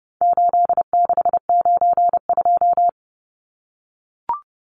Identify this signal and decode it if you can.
Morse code, heard as a keyed tone, about 30 wpm, 700 Hz. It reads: 8692
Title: Numbers - 4-Digit 30wpm